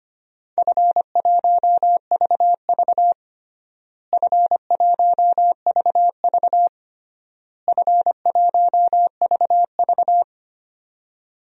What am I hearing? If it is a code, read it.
Morse code, heard as a keyed tone, about 25 wpm, 700 Hz. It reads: F1VV F1VV F1VV
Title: DXCC-100 Call Signs - 4 Characters - Repeated 3x - 25wpm